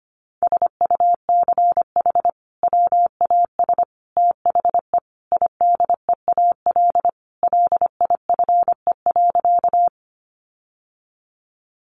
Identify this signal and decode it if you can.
Morse code, heard as a keyed tone, about 25 wpm, 700 Hz. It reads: SUCH WAS THE IDEAL LIFE.
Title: Sentences from Top 1400 Words 25wpm